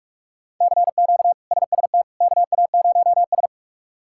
Morse code, heard as a keyed tone, about 43 wpm, 700 Hz. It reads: CQ SST KA0S
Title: K1USN SST QSO Practice - 45wpm